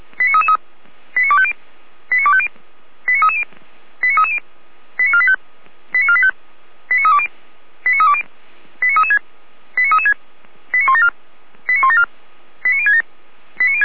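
Five-tone selective call (ZVEI), heard as quick runs of five tones, tones of about 70 ms. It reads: 89250 89250 89330 89330 89485 89485 89320 89320 89336 89336 89174 89174 89067
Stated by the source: ZVEI five-tone selective call 89327 received with an SDR.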